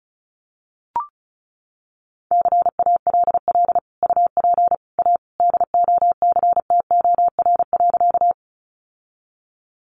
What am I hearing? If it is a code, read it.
Morse code, heard as a keyed tone, about 34 wpm, 700 Hz. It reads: CALL UP A DOCTOR.
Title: Sentences from Top 1500 Words 35wpm